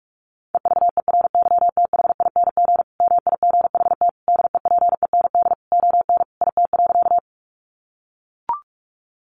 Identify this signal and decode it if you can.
Morse code, heard as a keyed tone, about 44 wpm, 700 Hz. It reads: EVERYTHING MIGHT DEPEND ON IT.
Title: Sentences from Top 1300 Words 45wpm